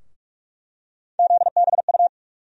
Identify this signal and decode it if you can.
Morse code, heard as a keyed tone, about 42 wpm, 700 Hz. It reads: ZBU
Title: USA Call Sign Postfixes - 3 Letters 45wpm